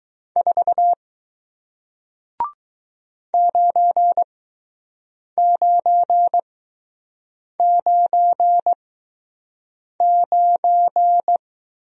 Morse code, heard as a keyed tone, about 20 wpm, spaced as for 6 wpm, 700 Hz. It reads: 4 9999